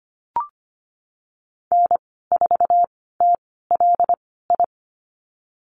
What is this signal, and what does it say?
Morse code, heard as a keyed tone, about 25 wpm, spaced as for 16 wpm, 700 Hz. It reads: N4TLI
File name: Lesson 15g - Number 5 - Call Signs 25wpm